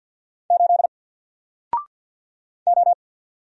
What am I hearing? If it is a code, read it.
Morse code, heard as a keyed tone, about 49 wpm, 700 Hz. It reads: 8 O
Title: Single Character (Letter:Number:Punctuation:Prosign) 50wpm